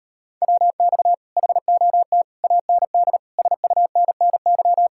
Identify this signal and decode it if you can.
Morse code, heard as a keyed tone, about 39 wpm, 700 Hz. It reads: WX HOT AND SUNNY